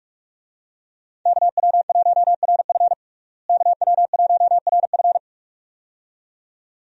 Morse code, heard as a keyed tone, about 42 wpm, 700 Hz. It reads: KW1RF KW1RF
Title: USA Call Signs - 5 Characters - Repeated 2x - 45wpm